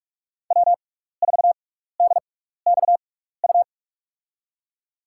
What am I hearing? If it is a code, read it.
Morse code, heard as a keyed tone, about 46 wpm, spaced as for 16 wpm, 700 Hz. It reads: W4DXU
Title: USA Call Signs - 5 Characters - 6x Character Spacing - 45wpm